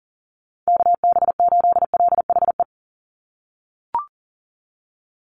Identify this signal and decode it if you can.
Morse code, heard as a keyed tone, about 39 wpm, 700 Hz. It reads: KB8LHE
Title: USA Call Signs 40wpm